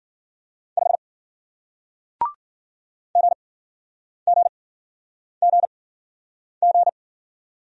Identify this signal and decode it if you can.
Morse code, heard as a keyed tone, about 53 wpm, spaced as for 9 wpm, 700 Hz. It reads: 5 GGGG